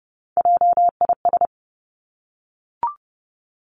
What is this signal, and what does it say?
Morse code, heard as a keyed tone, about 30 wpm, 700 Hz. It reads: JIS
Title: Three Challenging Letter-Number Combinations 30wpm